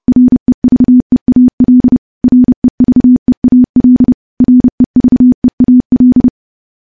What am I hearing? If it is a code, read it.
Morse code, heard as a keyed tone, about 30 wpm, 250 Hz. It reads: REVEAL REVEAL REVEAL
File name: morse-training-30WPM